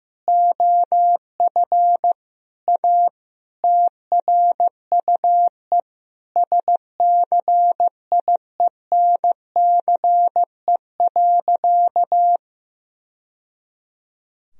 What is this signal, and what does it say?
Morse code, heard as a keyed tone, about 16 wpm, 700 Hz. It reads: OF A TRUE SCIENCE.
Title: Sentences from Top 1000 Words - Review Part 4 - 15wpm